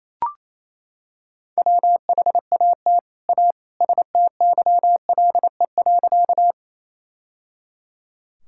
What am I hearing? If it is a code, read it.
Morse code, heard as a keyed tone, about 27 wpm, 700 Hz. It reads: WHAT A STYLE.
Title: Sentences from Top 1000 Words - Review Part 5 - 28wpm